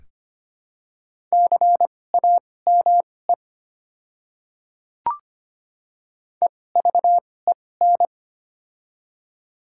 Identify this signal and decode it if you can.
Morse code, heard as a keyed tone, about 25 wpm, spaced as for 18 wpm, 700 Hz. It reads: CAME EVEN
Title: 2x Character Spacing - Top 200 Words - 25wpm